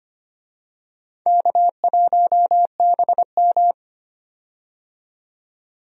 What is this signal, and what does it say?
Morse code, heard as a keyed tone, about 25 wpm, 700 Hz. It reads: K1BM
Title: USA Call Signs - 4 Characters - 25wpm